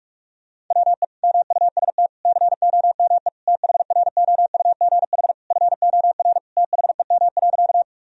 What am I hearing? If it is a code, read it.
Morse code, heard as a keyed tone, about 41 wpm, 700 Hz. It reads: WE MUST COME THROUGH FOR THEM.
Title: Sentences from Top 1000 Words - Review Part 1 - 45wpm